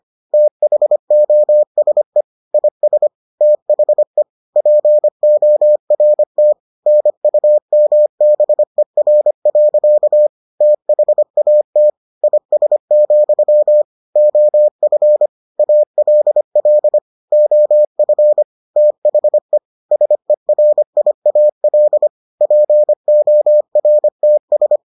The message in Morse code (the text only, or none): THOSE IS THE PORT NUMBER. THAT IS, OF ALL OF THE SERIAL PORTS